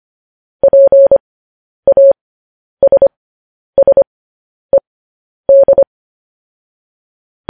Morse code, heard as a keyed tone, about 25 wpm, spaced as for 10 wpm, 550 Hz. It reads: PASSED